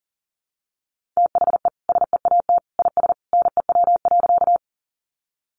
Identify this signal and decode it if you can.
Morse code, heard as a keyed tone, about 40 wpm, 700 Hz. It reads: THE SEAT IS NEW.
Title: Sentences from Top 1500 Words 40wpm